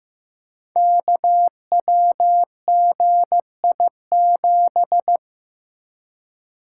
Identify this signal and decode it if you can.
Morse code, heard as a keyed tone, about 15 wpm, 700 Hz. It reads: KWGI7
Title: Code Groups - 5 Letter:Numbers - 15wpm